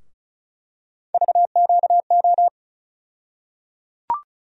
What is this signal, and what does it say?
Morse code, heard as a keyed tone, about 36 wpm, 700 Hz. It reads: VQO
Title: USA Call Sign Postfixes - 3 Letters 35wpm